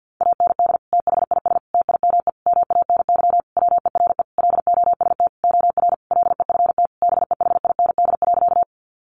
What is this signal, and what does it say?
Morse code, heard as a keyed tone, about 48 wpm, 700 Hz. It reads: AND THIS TIME MANY WERE LOST OR LEFT BEHIND.
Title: Sentences from Top 1400 Words 50wpm